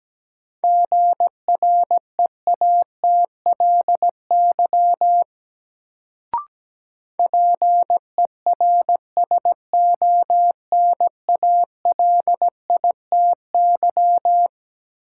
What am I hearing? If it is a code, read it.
Morse code, heard as a keyed tone, about 17 wpm, 700 Hz. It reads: GREATLY PERSONALITY
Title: Sentences from Top 1400 Words - New Word Drill Alt - 17wpm